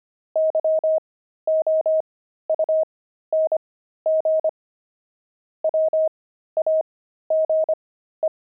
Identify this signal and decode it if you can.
Morse code, heard as a keyed tone, about 25 wpm, spaced as for 13 wpm, 630 Hz. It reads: YOUNG WAGE
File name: pangram-fs-13-cs-25-zeal